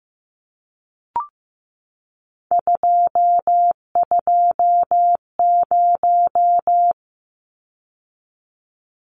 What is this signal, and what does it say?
Morse code, heard as a keyed tone, about 15 wpm, 700 Hz. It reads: 220
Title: Numbers - 3-Digit 15wpm